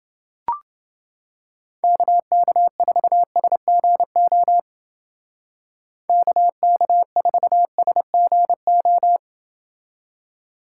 Morse code, heard as a keyed tone, about 29 wpm, 700 Hz. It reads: KK4SGO KK4SGO